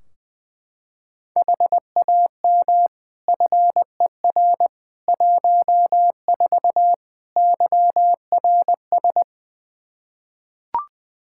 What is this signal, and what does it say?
Morse code, heard as a keyed tone, about 20 wpm, 700 Hz. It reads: HAM FER 14 YRS
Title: Practice QSO - Exchange 3 - More Info - 20wpm